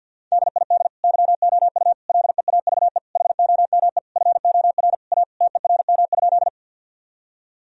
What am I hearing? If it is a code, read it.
Morse code, heard as a keyed tone, about 50 wpm, 700 Hz. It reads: DID YOU LEAVE HOME FOR A TERM?
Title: Sentences from Top 1000 Words - Review Part 4 - 50wpm